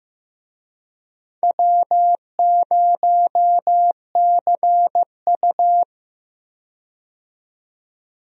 Morse code, heard as a keyed tone, about 15 wpm, 700 Hz. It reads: W0CU
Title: USA Call Signs - 4 Characters - 15wpm